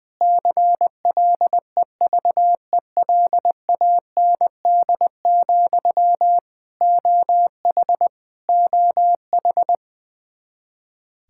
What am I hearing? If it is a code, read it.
Morse code, heard as a keyed tone, about 20 wpm, 700 Hz. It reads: CLEVELAND, OH OH
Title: Practice QSO - Exchange 1 - Beginning - 20wpm